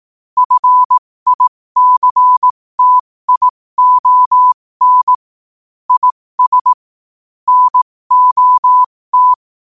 Morse code, heard as a keyed tone, about 18 wpm, spaced as for 15 wpm, 1000 Hz. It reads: FICTION IS NOT
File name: qotdfiles / 15wpm